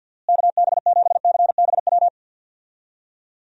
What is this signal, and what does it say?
Morse code, heard as a keyed tone, about 51 wpm, 700 Hz. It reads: KB7CBW
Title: USA Call Signs - 6 Characters - Repeated 3x - 50wpm